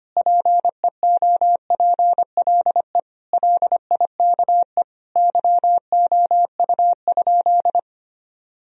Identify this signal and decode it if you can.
Morse code, heard as a keyed tone, about 25 wpm, 700 Hz. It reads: PEOPLE LIKE YOU?
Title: Sentences from Top 1000 Words - Review Part 1 - 25wpm